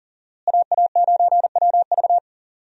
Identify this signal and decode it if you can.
Morse code, heard as a keyed tone, about 40 wpm, 700 Hz. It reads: AA9WV
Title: USA Call Signs - 5 Characters - Repeated 3x - 40wpm